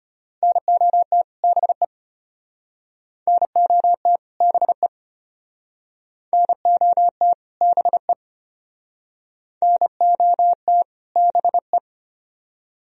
Morse code, heard as a keyed tone, about 34 wpm, 700 Hz. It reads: NOT BE NOT BE NOT BE NOT BE